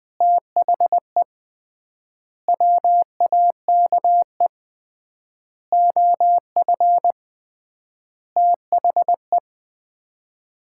Morse code, heard as a keyed tone, about 20 wpm, 700 Hz. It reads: THE WAKE OF THE
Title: 3x Word-Spacing - Sets of 5 Words - 20wpm